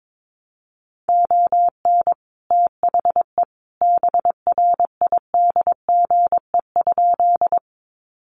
Morse code, heard as a keyed tone, about 22 wpm, 700 Hz. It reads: ON THE BRIDGE?
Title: Sentences from Top 1200 Words 22wpm